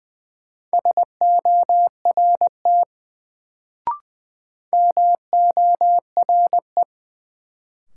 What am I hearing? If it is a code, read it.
Morse code, heard as a keyed tone, about 20 wpm, 700 Hz. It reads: SORT MORE
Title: Sets of 2 Words - New Word Drill - 20wpm